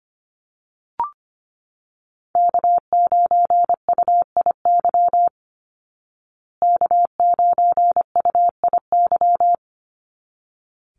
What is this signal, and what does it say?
Morse code, heard as a keyed tone, about 25 wpm, 700 Hz. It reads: K9UIY K9UIY
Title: Call Signs Repeated 3x - 25wpm